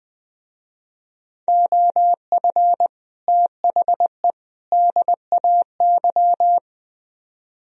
Morse code, heard as a keyed tone, about 21 wpm, 700 Hz. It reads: OF THE DAY
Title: Speed-Racing - Sets of 3 Words 20wpm